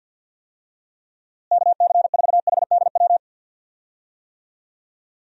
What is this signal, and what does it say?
Morse code, heard as a keyed tone, about 52 wpm, 700 Hz. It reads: KC4HDW
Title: USA Call Signs - 6 Characters - Repeated 2x - 50wpm